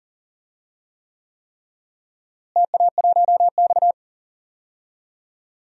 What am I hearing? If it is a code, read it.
Morse code, heard as a keyed tone, about 40 wpm, 700 Hz. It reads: TA1X
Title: DXCC-100 Call Signs - 4 Characters - Repeated 3x - 40wpm